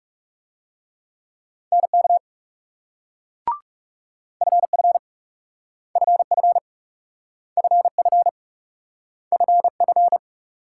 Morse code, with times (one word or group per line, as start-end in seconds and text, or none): 1.7-2.2 NK
4.4-5.0 FF
6.0-6.6 FF
7.6-8.3 FF
9.3-10.2 FF